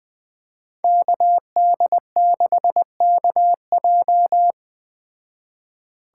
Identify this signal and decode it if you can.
Morse code, heard as a keyed tone, about 20 wpm, 700 Hz. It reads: KD6KJ
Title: Speed Racing - USA 2x2 Call Signs 17wpm